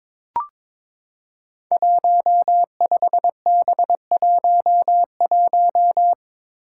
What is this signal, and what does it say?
Morse code, heard as a keyed tone, about 22 wpm, 700 Hz. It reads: 15B11
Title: Five Challenging Letter-Number Combinations 22wpm